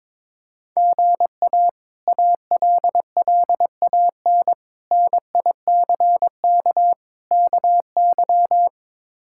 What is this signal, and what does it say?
Morse code, heard as a keyed tone, about 21 wpm, 700 Hz. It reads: GA ALLAN NICK KY
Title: K1USN SST QSO Practice - 22wpm